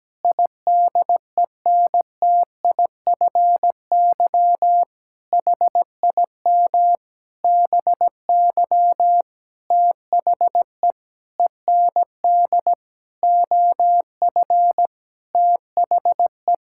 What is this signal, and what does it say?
Morse code, heard as a keyed tone, about 17 wpm, 700 Hz. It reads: IDENTIFY HIM BY THE END OF THE
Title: Sentences from Top 1000 Words - Review Part 6 - 17wpm